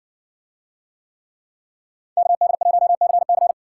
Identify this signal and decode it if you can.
Morse code, heard as a keyed tone, about 59 wpm, 700 Hz. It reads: KD1CC